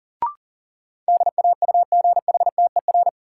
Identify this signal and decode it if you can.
Morse code, heard as a keyed tone, about 40 wpm, 700 Hz. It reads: DAUGHTER
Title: Sentences from Top 1500 Words - New Word Drill Alt - 40wpm